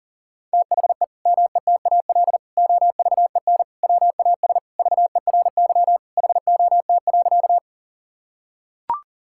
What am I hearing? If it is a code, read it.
Morse code, heard as a keyed tone, about 37 wpm, 700 Hz. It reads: THE METAL OVEN WAS VERY HOT.
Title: Sentences from Top 1000 Words - Review Part 6 - 40wpm